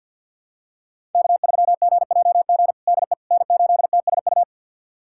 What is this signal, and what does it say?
Morse code, heard as a keyed tone, about 52 wpm, 690 Hz. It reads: K3GJG DE N8TSU
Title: USA Callsign DE Callsign - 5 Characters - 50wpm